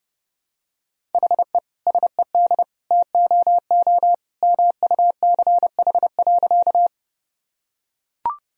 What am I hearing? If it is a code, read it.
Morse code, heard as a keyed tone, about 30 wpm, 700 Hz. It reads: HE SED TOO MUCH.
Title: Sentences from Top 200 Words and Abbreviations 30wpm